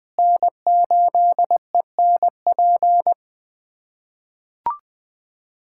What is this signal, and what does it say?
Morse code, heard as a keyed tone, about 20 wpm, 700 Hz. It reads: N8ENP
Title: USA Callsign DE Callsign - 5 Characters - 20wpm